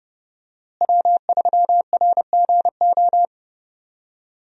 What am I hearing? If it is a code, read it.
Morse code, heard as a keyed tone, about 30 wpm, 700 Hz. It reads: W3RGO
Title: USA Call Signs - 5 Characters - Repeated 3x - 30wpm